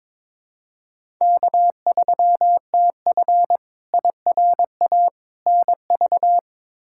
Morse code, heard as a keyed tone, about 22 wpm, 700 Hz. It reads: K3TF IRA NV
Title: K1USN SST QSO Practice - 22wpm